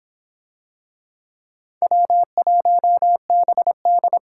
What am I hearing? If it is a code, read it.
Morse code, heard as a keyed tone, about 25 wpm, 700 Hz. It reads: W1BD